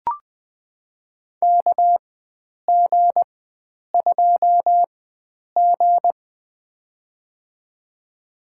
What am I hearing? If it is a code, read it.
Morse code, heard as a keyed tone, about 20 wpm, spaced as for 9 wpm, 700 Hz. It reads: KG2G
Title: USA Call Signs - 4 Characters - 4x Character Spacing - 20wpm